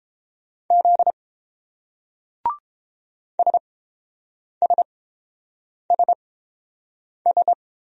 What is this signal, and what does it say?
Morse code, heard as a keyed tone, about 30 wpm, spaced as for 7 wpm, 700 Hz. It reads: Z SSSS